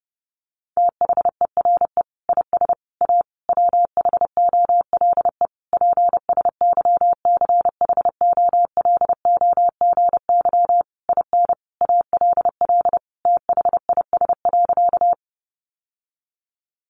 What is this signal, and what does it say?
Morse code, heard as a keyed tone, about 30 wpm, 700 Hz. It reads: THERE IS A WHOLE PSYCHOLOGY IN ALL THIS.
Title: Sentences from Top 1400 Words 30wpm